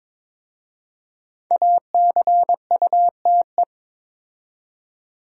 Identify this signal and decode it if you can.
Morse code, heard as a keyed tone, about 21 wpm, 700 Hz. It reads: ACUTE
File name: 5 Letter Words - First 3 Letters Given - 22wpm